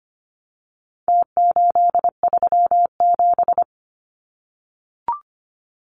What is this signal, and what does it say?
Morse code, heard as a keyed tone, about 25 wpm, 700 Hz. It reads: T837
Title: Numbers - Contest Style - 4-Digit 25wpm